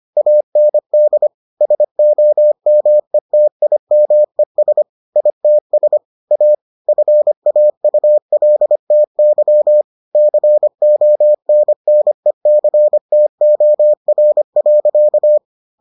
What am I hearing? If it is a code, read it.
Morse code, heard as a keyed tone, about 25 wpm, 600 Hz. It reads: AND SOMETIMES ITS A FAULTY CONNECTOR.